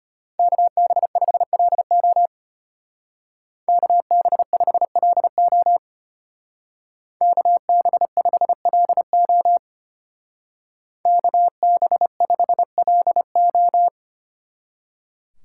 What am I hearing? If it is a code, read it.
Morse code, heard as a keyed tone, about 32 wpm, 700 Hz. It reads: KB5LO KB5LO KB5LO KB5LO